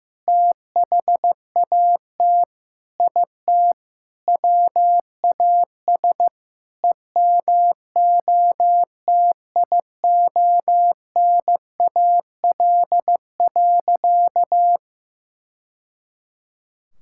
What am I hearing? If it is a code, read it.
Morse code, heard as a keyed tone, about 15 wpm, 700 Hz. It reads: THAT IT WAS EMOTIONAL.
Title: Sentences from Top 1000 Words - Review Part 5 - 15wpm